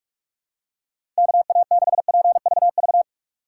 Morse code, heard as a keyed tone, about 45 wpm, 700 Hz. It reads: KA6PVV
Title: USA Call Signs - 6 Characters - 3x Character Spacing - 45wpm